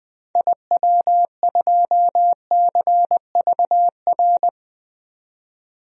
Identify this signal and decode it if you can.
Morse code, heard as a keyed tone, about 21 wpm, 700 Hz. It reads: IW2CVR
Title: DXCC-100 Call Signs - 6 Characters - 20wpm